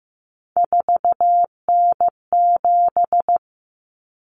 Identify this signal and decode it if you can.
Morse code, heard as a keyed tone, about 15 wpm, 700 Hz. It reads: 4N7
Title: Common Cut Numbers - 3-Digit 15wpm